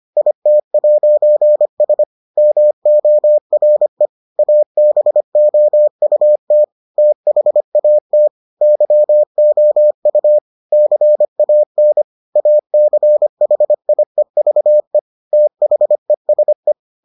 Morse code, heard as a keyed tone, about 25 wpm, 600 Hz. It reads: IT'S MORE ABOUT THAT YOU CAN ACHIEVE THESE